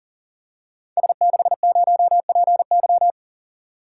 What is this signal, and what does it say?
Morse code, heard as a keyed tone, about 42 wpm, 700 Hz. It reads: S60PY
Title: Code Groups - 5 Letter:Numbers - 40wpm